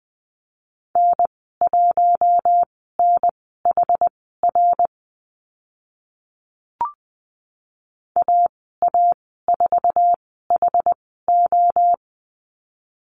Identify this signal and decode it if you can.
Morse code, heard as a keyed tone, about 20 wpm, spaced as for 14 wpm, 700 Hz. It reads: N1NHR AA4HO
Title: 082 Lesson 14g - Number 2 - Call Signs 20wpm